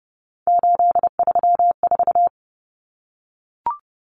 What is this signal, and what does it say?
Morse code, heard as a keyed tone, about 30 wpm, 700 Hz. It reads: 834
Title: Numbers - 3-Digit 30wpm